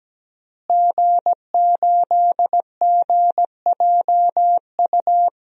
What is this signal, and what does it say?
Morse code, heard as a keyed tone, about 17 wpm, 700 Hz. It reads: G8GJU
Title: DXCC-100 Call Signs - 5 Characters - Repeated 2x - 17wpm